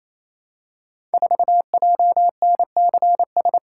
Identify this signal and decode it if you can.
Morse code, heard as a keyed tone, about 27 wpm, 700 Hz. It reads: 4JNCS
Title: Code Groups - 5 Letter:Numbers - 28wpm